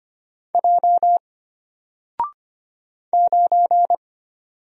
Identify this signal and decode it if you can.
Morse code, heard as a keyed tone, about 25 wpm, 700 Hz. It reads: J 9
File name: Single-Letter-Number-25wpm